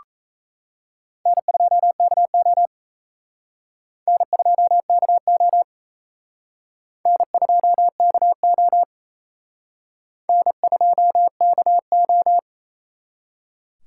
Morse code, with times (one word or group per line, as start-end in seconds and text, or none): 1.3-2.7 N2KO
4.1-5.6 N2KO
7.1-8.8 N2KO
10.3-12.4 N2KO